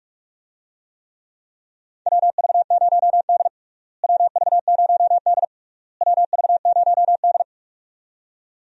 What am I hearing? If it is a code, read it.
Morse code, heard as a keyed tone, about 41 wpm, 700 Hz. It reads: WV0D WV0D WV0D
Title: USA Call Signs - 4 Characters - Repeated 3x - 45wpm